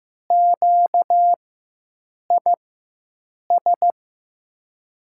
Morse code, heard as a keyed tone, about 15 wpm, spaced as for 7 wpm, 700 Hz. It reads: QIS
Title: USA Call Signs - 6 Characters - 4x Character Spacing - 15wpm